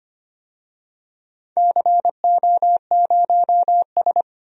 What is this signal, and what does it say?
Morse code, heard as a keyed tone, about 25 wpm, 700 Hz. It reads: CO0S
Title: DXCC-100 Call Signs - 4 Characters - 25wpm